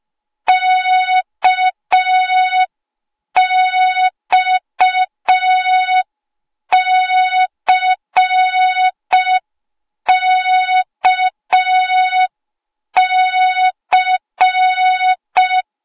Morse code, heard as a keyed tone, about 5 wpm, 3700 Hz. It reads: KXCKC